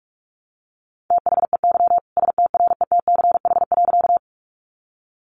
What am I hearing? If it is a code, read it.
Morse code, heard as a keyed tone, about 44 wpm, 700 Hz. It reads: THEY STRETCH.
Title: Easy Phrases 45wpm